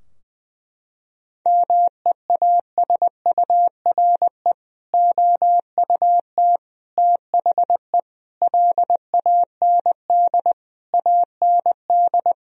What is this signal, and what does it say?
Morse code, heard as a keyed tone, about 21 wpm, 700 Hz. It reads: MEASURE OUT THE LAND AND
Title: Sentences from Top 1000 Words - Review Part 2 - 20wpm